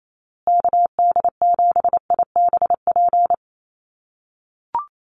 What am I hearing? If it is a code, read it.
Morse code, heard as a keyed tone, about 28 wpm, 700 Hz. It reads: KD7IBP
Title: USA Call Signs Repeated 3x 28wpm